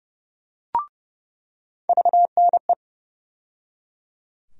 Morse code, heard as a keyed tone, about 30 wpm, 700 Hz. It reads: VNE
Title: USA Call Sign Postfixes - 3 Letters 30wpm